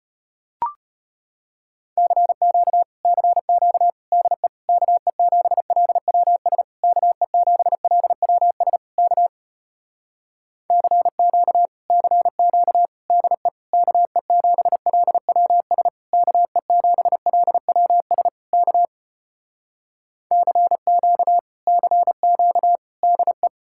Morse code, heard as a keyed tone, about 33 wpm, 700 Hz. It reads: CQ CQ DE KE7LWS KE7LWS K CQ CQ DE KE7LWS KE7LWS K CQ CQ DE